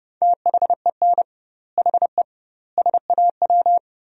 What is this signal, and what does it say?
Morse code, heard as a keyed tone, about 32 wpm, 700 Hz. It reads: THEN HE SAW
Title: 2x Word-Spacing - Sentences from Top 400 Words - 30wpm